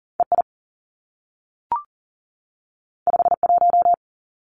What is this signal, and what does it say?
Morse code, heard as a keyed tone, about 39 wpm, 700 Hz. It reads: EI 51
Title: Two Challenging Letter-Number Combinations 40wpm